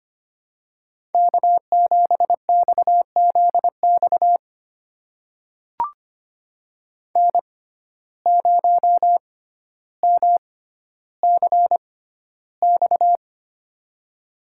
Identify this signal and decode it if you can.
Morse code, heard as a keyed tone, about 25 wpm, spaced as for 13 wpm, 700 Hz. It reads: K7XZX N0MCX
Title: USA Call Signs - 5 Characters - 6x Character Spacing - 25wpm